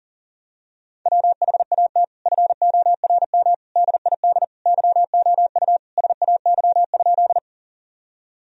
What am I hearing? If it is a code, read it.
Morse code, heard as a keyed tone, about 40 wpm, 700 Hz. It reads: WHAT FORM DID YOU SAY?
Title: Sentences from Top 1000 Words - Review Part 1 - 40wpm